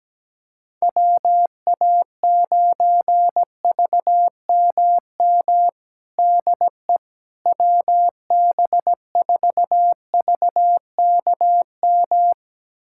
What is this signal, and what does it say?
Morse code, heard as a keyed tone, about 17 wpm, 700 Hz. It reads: WA9VMM DE WB4VKM
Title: USA Callsign DE Callsign - 6 Characters - 17wpm